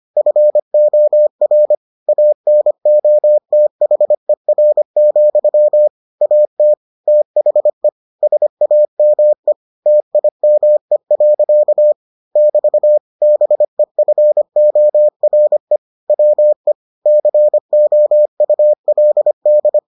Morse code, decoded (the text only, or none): FOR ANOTHER, AT THE SAME TIME. <BT> BEFORE WE COULD